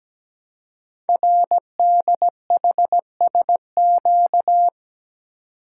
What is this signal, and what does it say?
Morse code, heard as a keyed tone, about 17 wpm, 700 Hz. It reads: RDHSQ
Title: Code Groups - 5 Letters - 17wpm